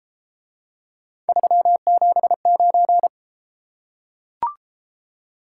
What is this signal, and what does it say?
Morse code, heard as a keyed tone, about 33 wpm, 700 Hz. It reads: 379